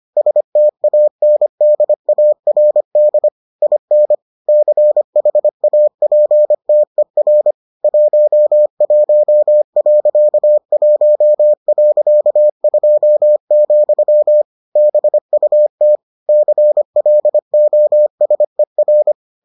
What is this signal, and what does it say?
Morse code, heard as a keyed tone, about 25 wpm, 600 Hz. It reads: STANDARD IN CHAPTER 11.1.2, BUT CLOSER